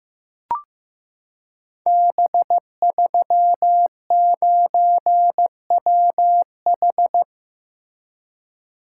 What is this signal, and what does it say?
Morse code, heard as a keyed tone, about 15 wpm, 700 Hz. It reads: B39WH
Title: Code Groups - 5 Letter:Numbers - 15wpm